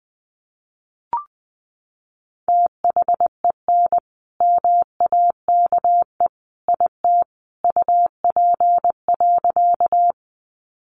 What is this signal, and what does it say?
Morse code, heard as a keyed tone, about 20 wpm, 700 Hz. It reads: THEN MAKE IT UP.
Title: Sentences from Top 100 Words 20wpm